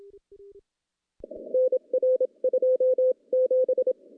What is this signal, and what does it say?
Morse code, heard as a keyed tone, about 27 wpm, spaced as for 23 wpm, 510 Hz. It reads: NR 27
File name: contests / CW-SS-k5tr- -0100